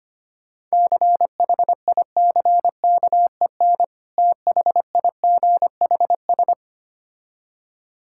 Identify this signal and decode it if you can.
Morse code, heard as a keyed tone, about 25 wpm, 700 Hz. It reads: CHICKEN THIGHS
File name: practice / 25wpm / Difficult Words with Hints - 25wpm